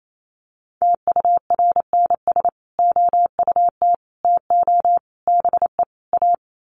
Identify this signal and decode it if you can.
Morse code, heard as a keyed tone, about 28 wpm, 700 Hz. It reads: TURNS OUT TO BE A
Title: Sets of 5 Words Encore 28wpm